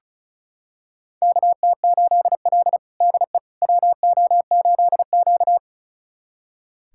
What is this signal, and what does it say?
Morse code, heard as a keyed tone, about 36 wpm, 700 Hz. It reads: KT8L DE WO8Q
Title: USA Callsign DE Callsign - 4 Characters - 35wpm